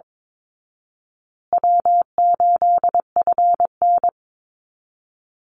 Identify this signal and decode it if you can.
Morse code, heard as a keyed tone, about 22 wpm, 700 Hz. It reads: W8FN
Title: Call Signs Repeated 2x - 22wpm